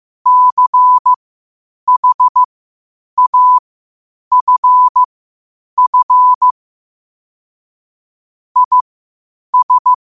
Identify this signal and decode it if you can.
Morse code, heard as a keyed tone, about 15 wpm, spaced as for 8 wpm, 1000 Hz. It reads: CHAFF IS
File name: qotdfiles / 8wpm